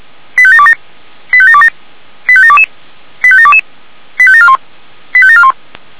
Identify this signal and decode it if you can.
Five-tone selective call (ZVEI), heard as quick runs of five tones, tones of about 70 ms. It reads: 85727 85727 85722 85722 85741 85741
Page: /464/SDR/ZVEI/